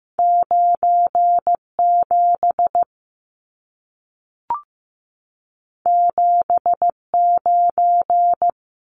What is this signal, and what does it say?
Morse code, heard as a keyed tone, about 15 wpm, 700 Hz. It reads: 97 79
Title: Numbers - 2-Digit 15wpm